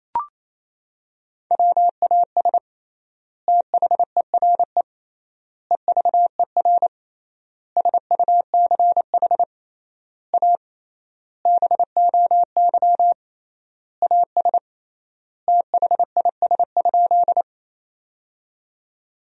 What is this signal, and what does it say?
Morse code, heard as a keyed tone, about 28 wpm, 700 Hz. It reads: WAS THERE EVER SUCH A BOY AS THIS?
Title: 3x Word-Spacing - Sentences from Top 400 Words - 28wpm